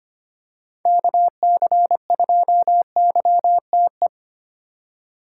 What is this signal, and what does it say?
Morse code, heard as a keyed tone, about 25 wpm, 700 Hz. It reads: KC2YTE
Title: USA Call Signs - 6 Characters - 8x Character Spacing - 25wpm